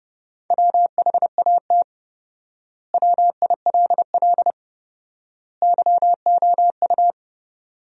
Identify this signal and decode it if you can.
Morse code, heard as a keyed tone, about 30 wpm, 700 Hz. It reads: WHAT WILL YOU
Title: 4x Word-Spacing - Sentences from Top 100 Words - 30wpm